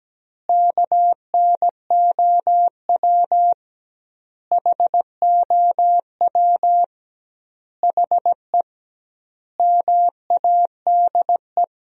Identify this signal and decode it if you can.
Morse code, heard as a keyed tone, about 17 wpm, 700 Hz. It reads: KNOW HOW HE MADE
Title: 2x Word-Spacing - Sentences from Top 300 Words - 17wpm